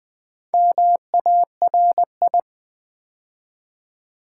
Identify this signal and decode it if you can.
Morse code, heard as a keyed tone, about 20 wpm, 700 Hz. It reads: MARI
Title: Build A Word - 2 Letters At A Time - 6-Letter Words - 20wpm